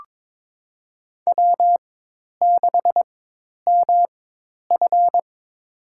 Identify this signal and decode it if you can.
Morse code, heard as a keyed tone, about 22 wpm, spaced as for 10 wpm, 700 Hz. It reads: W6MF
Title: USA Call Signs - 4 Characters - 4x Character Spacing - 22wpm